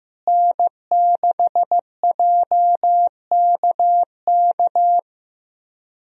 Morse code, heard as a keyed tone, about 15 wpm, 700 Hz. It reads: N6JKK
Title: USA Call Signs - 5 Characters - Repeated 3x - 15wpm